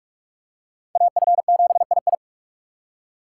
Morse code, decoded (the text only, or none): AF7II